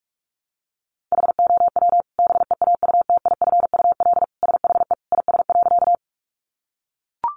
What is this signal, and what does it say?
Morse code, heard as a keyed tone, about 44 wpm, 700 Hz. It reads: HOW BEAUTIFUL SHE IS.
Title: Sentences from Top 900 Words 45wpm